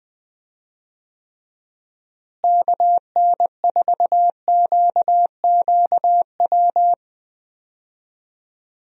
Morse code, heard as a keyed tone, about 20 wpm, 700 Hz. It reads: KN4QQW
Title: USA Call Signs - 6 Characters - Repeated 2x - 20wpm